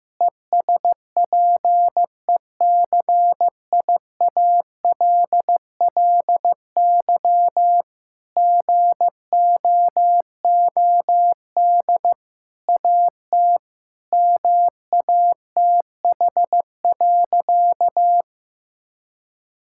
Morse code, heard as a keyed tone, about 15 wpm, 700 Hz. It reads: ESPECIALLY GOOD AT MATH.
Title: Sentences from Top 1000 Words - Review Part 6 - 15wpm